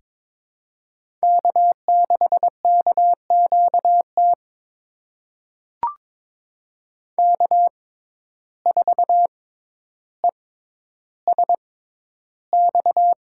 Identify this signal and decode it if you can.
Morse code, heard as a keyed tone, about 23 wpm, spaced as for 11 wpm, 700 Hz. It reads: K6KQT K4ESX
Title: USA Call Signs - 5 Characters - 6x Character Spacing - 22wpm